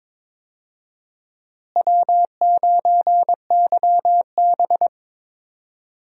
Morse code, heard as a keyed tone, about 22 wpm, 700 Hz. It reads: W9YB
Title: USA Call Signs - 4 Characters - Repeated 2x - 22wpm